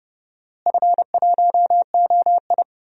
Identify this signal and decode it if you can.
Morse code, heard as a keyed tone, about 32 wpm, 700 Hz. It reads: F1OI
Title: DXCC-100 Call Signs - 4 Characters - Repeated 3x - 30wpm